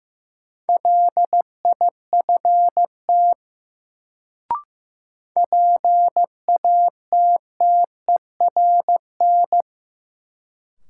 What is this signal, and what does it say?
Morse code, heard as a keyed tone, about 15 wpm, 700 Hz. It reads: LIFT PATTERN
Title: Sentences from Top 1000 Words - New Word Drill - 15wpm